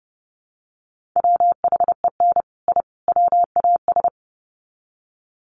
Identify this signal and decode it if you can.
Morse code, heard as a keyed tone, about 30 wpm, 700 Hz. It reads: WHEN I WAS
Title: Sets of 3 Words 30wpm